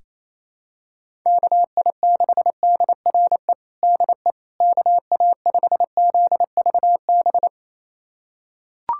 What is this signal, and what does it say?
Morse code, heard as a keyed tone, about 28 wpm, 700 Hz. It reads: KI6DRE DE KA5ZVB
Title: USA Callsign DE Callsign 28wpm